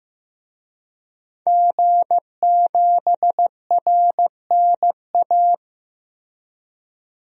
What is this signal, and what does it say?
Morse code, heard as a keyed tone, about 16 wpm, 700 Hz. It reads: G7RNA
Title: DXCC-100 Call Signs - 5 Characters - 15wpm